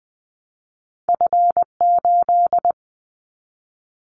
Morse code, heard as a keyed tone, about 20 wpm, 700 Hz. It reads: F8
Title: Callsign Prefixes for Top 25 DXCC - 20wpm